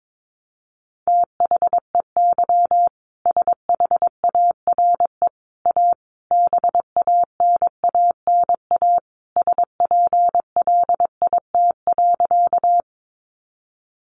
Morse code, heard as a keyed tone, about 22 wpm, 700 Hz. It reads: THEY SHARE A BANANA SPLIT.
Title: Additional Easy Phrases 22wpm